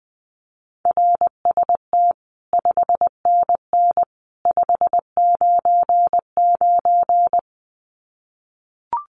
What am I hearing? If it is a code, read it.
Morse code, heard as a keyed tone, about 20 wpm, 700 Hz. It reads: RST 5NN 599
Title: Common QSO Phrases 20wpm